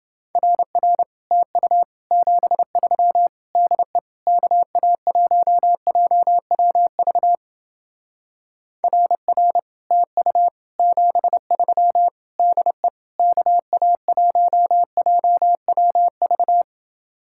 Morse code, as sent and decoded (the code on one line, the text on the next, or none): .-. .-. / - ..- / --... ...-- / -.. / -.- .- .---- .--- .-- ...- / .-. .-. / - ..- / --... ...-- / -.. / -.- .- .---- .--- .-- ...-
RR TU 73 DE KA1JWV RR TU 73 DE KA1JWV